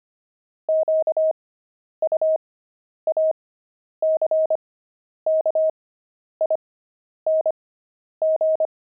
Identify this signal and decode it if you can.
Morse code, heard as a keyed tone, about 25 wpm, spaced as for 10 wpm, 630 Hz. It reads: QUACKING